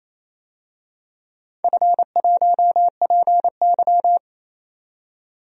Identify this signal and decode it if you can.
Morse code, heard as a keyed tone, about 29 wpm, 700 Hz. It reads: F1PY
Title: DXCC-100 Call Signs - 4 Characters - 28wpm